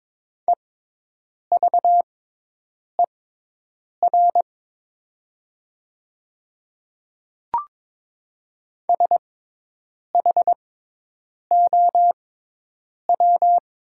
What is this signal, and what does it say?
Morse code, heard as a keyed tone, about 22 wpm, spaced as for 8 wpm, 700 Hz. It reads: EVER SHOW
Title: 6x Character Spacing - Top 500 Words - 22wpm